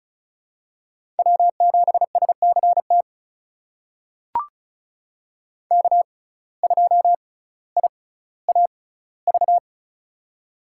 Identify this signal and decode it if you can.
Morse code, heard as a keyed tone, about 36 wpm, spaced as for 18 wpm, 700 Hz. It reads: W7SCT K2IAV
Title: USA Call Signs - 5 Characters - 6x Character Spacing - 35wpm